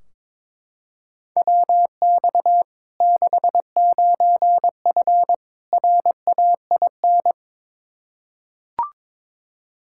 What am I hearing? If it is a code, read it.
Morse code, heard as a keyed tone, about 22 wpm, 700 Hz. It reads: WX 69F RAIN
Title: Practice QSO - Exchange 2 - Info - 22wpm